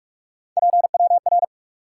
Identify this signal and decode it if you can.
Morse code, heard as a keyed tone, about 46 wpm, 700 Hz. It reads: PWR
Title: Top 180 QSO Elements 45wpm